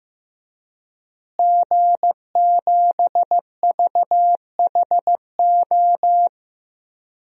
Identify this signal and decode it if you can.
Morse code, heard as a keyed tone, about 15 wpm, 700 Hz. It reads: G7VHO
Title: DXCC-100 Call Signs - 5 Characters - Repeated 2x - 15wpm